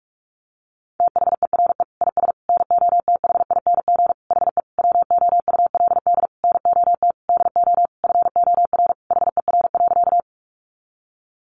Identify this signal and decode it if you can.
Morse code, heard as a keyed tone, about 44 wpm, 700 Hz. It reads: THERE IS NOTHING HE WOULD NOT DO FOR HER.
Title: Sentences from Top 600 Words 45wpm